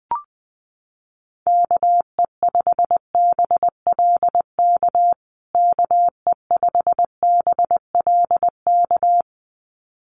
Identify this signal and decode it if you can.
Morse code, heard as a keyed tone, about 20 wpm, 700 Hz. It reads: KE5BLK KE5BLK
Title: USA Call Signs Repeated 2x 20wpm